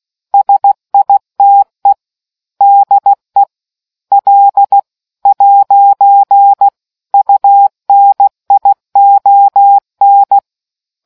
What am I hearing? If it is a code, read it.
Morse code, heard as a keyed tone, about 16 wpm, 800 Hz. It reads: SITE DE L'UNION